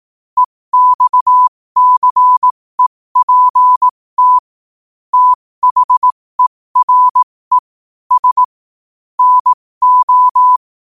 Morse code, heard as a keyed tone, about 18 wpm, spaced as for 15 wpm, 1000 Hz. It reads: EXCEPT THERE S NO
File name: qotdfiles / 15wpm